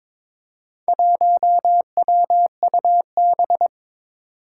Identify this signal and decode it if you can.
Morse code, heard as a keyed tone, about 22 wpm, 700 Hz. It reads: 1WUB
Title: Four Challenging Letter-Number Combinations 22wpm